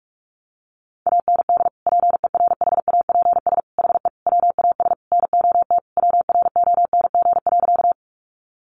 Morse code, heard as a keyed tone, about 44 wpm, 700 Hz. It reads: AND PERHAPS HE WAS NOT WRONG.
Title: Sentences from Top 900 Words 45wpm